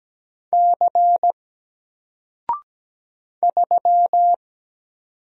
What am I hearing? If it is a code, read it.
Morse code, heard as a keyed tone, about 17 wpm, 700 Hz. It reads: C 3
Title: Single Letter-Number Alt 17wpm